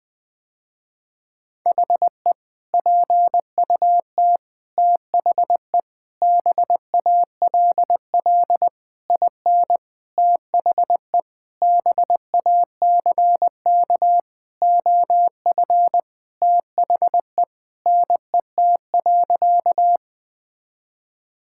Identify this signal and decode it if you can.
Morse code, heard as a keyed tone, about 20 wpm, 700 Hz. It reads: HE PUT THE BALL IN THE BACK OF THE NET.
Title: Sentences with 8-Word Phrases - Spoken Beginning - 20wpm